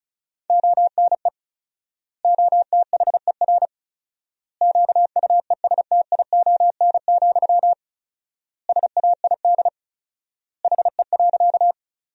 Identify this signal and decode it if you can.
Morse code, heard as a keyed tone, about 36 wpm, 700 Hz. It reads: ONE OTHER QUESTION, SAID HE.
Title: 4x Word-Spacing - Sentences from Top 400 Words - 35wpm